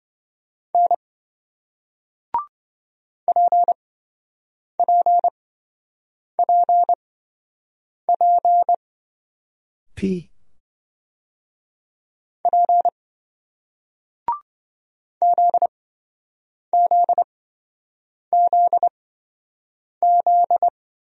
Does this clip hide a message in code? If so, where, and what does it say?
Yes: Morse code from 0.7 to 20.7 s, heard as a keyed tone, about 26 wpm, spaced as for 7 wpm, 700 Hz. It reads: N PPPP P ZZZZ